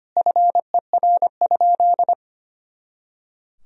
Morse code, heard as a keyed tone, about 25 wpm, 700 Hz. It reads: FER?
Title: Sentences from Top 100 Words and Abbreviations 25wpm